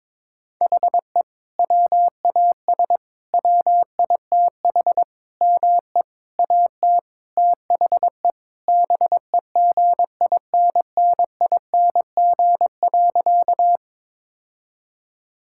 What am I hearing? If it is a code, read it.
Morse code, heard as a keyed tone, about 21 wpm, 700 Hz. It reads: HE WAS WITH ME AT THE BEGINNING.
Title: Sentences from Top 1000 Words - Review Part 5 - 22wpm